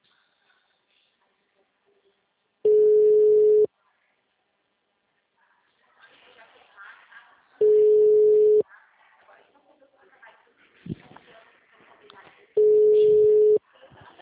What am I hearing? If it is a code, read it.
Ringback tone (Europe, 425 Hz), with speech.